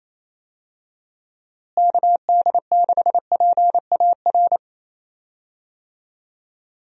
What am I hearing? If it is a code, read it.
Morse code, heard as a keyed tone, about 28 wpm, 700 Hz. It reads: KD6PAR
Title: USA Call Signs - 6 Characters - Repeated 3x - 28wpm